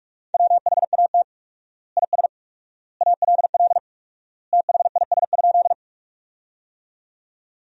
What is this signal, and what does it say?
Morse code, heard as a keyed tone, about 47 wpm, 700 Hz. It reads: WHAT IS ALL THIS?
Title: 4x Word-Spacing - Sentences from Top 100 Words - 45wpm